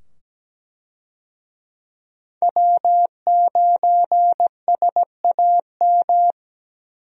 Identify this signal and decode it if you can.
Morse code, heard as a keyed tone, about 17 wpm, 700 Hz. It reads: W9SAM
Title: USA Call Signs - 5 Characters - 17wpm